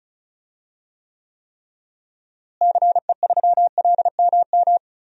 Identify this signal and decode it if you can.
Morse code, heard as a keyed tone, about 34 wpm, 700 Hz. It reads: CE3LMM
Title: DXCC-100 Call Signs - 6 Characters - Repeated 3x - 35wpm